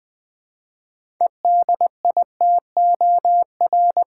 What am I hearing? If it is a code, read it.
Morse code, heard as a keyed tone, about 20 wpm, 690 Hz. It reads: EDITOR
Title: 6 Letter Words - First 3 Letters Given - 20wpm